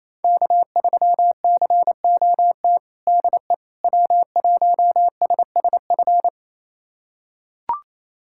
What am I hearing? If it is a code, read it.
Morse code, heard as a keyed tone, about 28 wpm, 700 Hz. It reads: K3COT DE W1SSF
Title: USA Callsign DE Callsign 28wpm